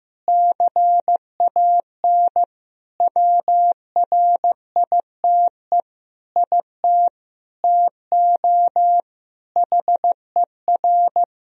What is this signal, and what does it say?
Morse code, heard as a keyed tone, about 15 wpm, 700 Hz. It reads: CAN WRITE IT TO HER
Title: Sentences from Top 1000 Words - Review Part 4 - 15wpm